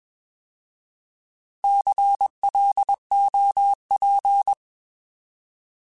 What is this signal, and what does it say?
Morse code, heard as a keyed tone, about 21 wpm, 800 Hz. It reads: CLOP
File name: Alphabetical / Alphabetical mp3s / Cs @20